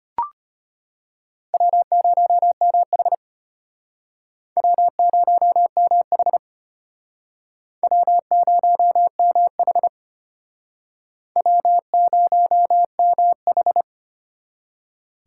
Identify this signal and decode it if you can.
Morse code, heard as a keyed tone, about 33 wpm, 700 Hz. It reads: W0MH W0MH W0MH W0MH